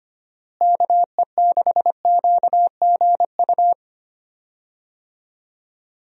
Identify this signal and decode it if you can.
Morse code, heard as a keyed tone, about 25 wpm, 700 Hz. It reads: KE6QGU
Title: USA Call Signs - 6 Characters - Repeated 3x - 25wpm